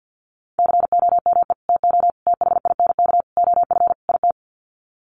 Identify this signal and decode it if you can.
Morse code, heard as a keyed tone, about 50 wpm, 700 Hz. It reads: COME TO THINK OF IT
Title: Sets of 5 Words Encore 50wpm